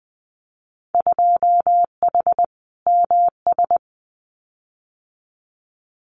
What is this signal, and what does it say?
Morse code, heard as a keyed tone, about 20 wpm, 700 Hz. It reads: 2H MS
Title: Winter Field Day Exchanges 20wpm